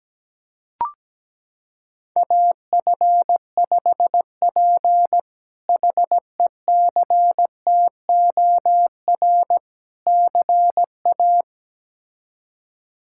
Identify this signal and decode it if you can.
Morse code, heard as a keyed tone, about 17 wpm, 700 Hz. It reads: AF5P HECTOR CA
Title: K1USN SST QSO Practice - 17wpm